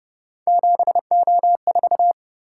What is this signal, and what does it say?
Morse code, heard as a keyed tone, about 28 wpm, 700 Hz. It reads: 7O4